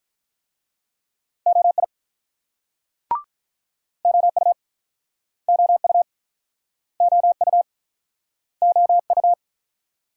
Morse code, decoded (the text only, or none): OI OU OU OU OU